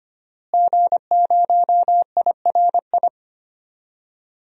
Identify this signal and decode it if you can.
Morse code, heard as a keyed tone, about 25 wpm, 700 Hz. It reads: G0IRI
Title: DXCC-100 Call Signs - 5 Characters - Repeated 3x - 25wpm